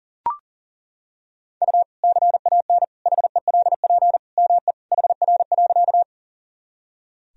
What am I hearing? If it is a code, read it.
Morse code, heard as a keyed tone, about 40 wpm, 700 Hz. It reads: U CAN HELP ME HR.
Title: Sentences from Top 200 Words and Abbreviations 40wpm